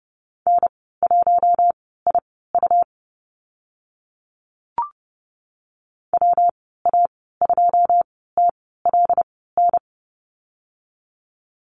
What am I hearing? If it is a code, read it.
Morse code, heard as a keyed tone, about 30 wpm, spaced as for 17 wpm, 700 Hz. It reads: N1IU WA2TLN
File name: Lesson 19g - Letter w - Call Signs 30wpm